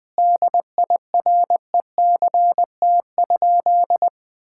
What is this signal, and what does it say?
Morse code, heard as a keyed tone, about 21 wpm, 700 Hz. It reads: DIRECT?
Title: Common QSO Phrases - 4x Word-Spacing - 20wpm